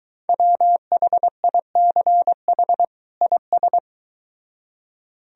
Morse code, heard as a keyed tone, about 23 wpm, 700 Hz. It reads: WHICH IS